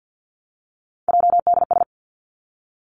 Morse code, heard as a keyed tone, about 50 wpm, 700 Hz. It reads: JDS
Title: Three Challenging Letter-Number Combinations 50wpm